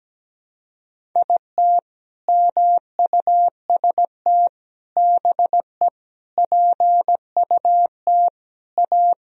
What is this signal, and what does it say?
Morse code, heard as a keyed tone, about 17 wpm, 700 Hz. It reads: IT MUST BE PUT A
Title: Sentences from Top 1000 Words - Review Part 1 - 17wpm